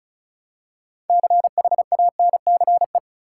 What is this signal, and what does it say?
Morse code, heard as a keyed tone, about 34 wpm, 700 Hz. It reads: CHANCE
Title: Six Letter Words - 35wpm